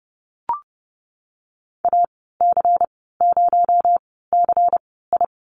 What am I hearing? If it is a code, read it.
Morse code, heard as a keyed tone, about 30 wpm, spaced as for 17 wpm, 700 Hz. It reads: AC0CI
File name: Lesson 37g - Letter x - Call Signs 30wpm